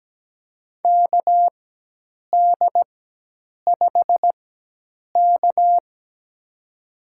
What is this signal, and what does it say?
Morse code, heard as a keyed tone, about 17 wpm, spaced as for 8 wpm, 700 Hz. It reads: KD5K
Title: USA Call Signs - 4 Characters - 4x Character Spacing - 17wpm